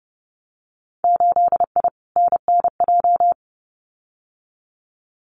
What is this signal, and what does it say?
Morse code, heard as a keyed tone, about 30 wpm, 700 Hz. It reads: 8I NNJ
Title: Winter Field Day Exchanges Encore 30wpm